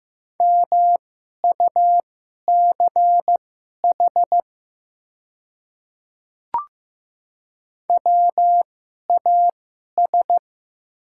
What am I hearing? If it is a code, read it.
Morse code, heard as a keyed tone, about 15 wpm, spaced as for 11 wpm, 700 Hz. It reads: MUCH WAS
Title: 2x Character Spacing - Top 500 Words - 15wpm